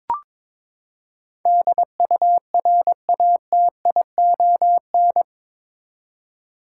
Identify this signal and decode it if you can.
Morse code, heard as a keyed tone, about 22 wpm, 700 Hz. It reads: DURATION
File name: practice / 22wpm / Eight Letter Words - 22wpm